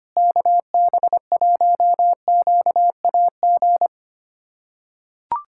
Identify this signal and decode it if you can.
Morse code, heard as a keyed tone, about 25 wpm, 700 Hz. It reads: KB1QAG
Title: USA Callsign DE Callsign 25wpm